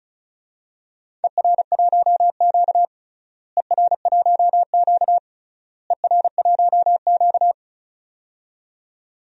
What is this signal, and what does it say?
Morse code, heard as a keyed tone, about 36 wpm, 700 Hz. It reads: ER1Q ER1Q ER1Q
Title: DXCC-100 Call Signs - 4 Characters - Repeated 3x - 35wpm